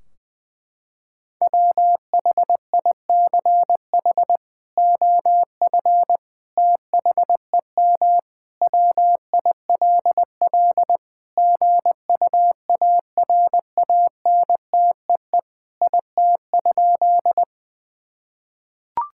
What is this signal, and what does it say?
Morse code, heard as a keyed tone, about 21 wpm, 700 Hz. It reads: WHICH OF THEM WILL GUARANTEE IT?
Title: Sentences from Top 1600 Words 20wpm